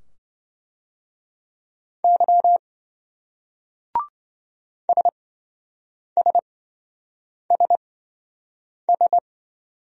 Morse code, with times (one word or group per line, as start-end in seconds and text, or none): 2.0-2.6 Y
4.9-9.2 SSSS